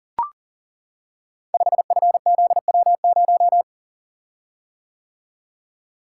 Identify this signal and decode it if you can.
Morse code, heard as a keyed tone, about 39 wpm, 700 Hz. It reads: 5FZW0
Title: Code Groups - 5 Letter:Numbers - 40wpm